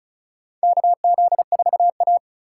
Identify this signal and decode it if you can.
Morse code, heard as a keyed tone, about 34 wpm, 700 Hz. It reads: KZ4A
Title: K1USN SST QSO Practice - 35wpm